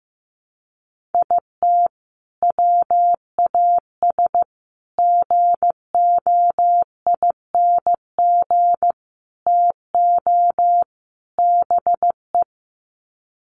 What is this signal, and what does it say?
Morse code, heard as a keyed tone, about 15 wpm, 700 Hz. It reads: IT WAS GOING TO BE
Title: Sets of 5 Words 15wpm